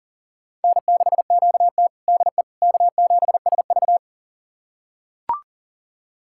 Morse code, heard as a keyed tone, about 37 wpm, 700 Hz. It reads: N6QT DE K7SV
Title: USA Callsign DE Callsign - 4 Characters - 40wpm